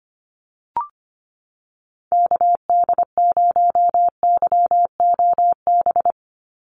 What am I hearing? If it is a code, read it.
Morse code, heard as a keyed tone, about 25 wpm, 700 Hz. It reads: KD0YOB
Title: NA SOTA Activator Callsigns 25wpm